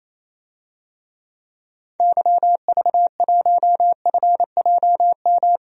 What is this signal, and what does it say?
Morse code, heard as a keyed tone, about 28 wpm, 700 Hz. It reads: YV1FJM
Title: DXCC-100 Call Signs - 6 Characters - 28wpm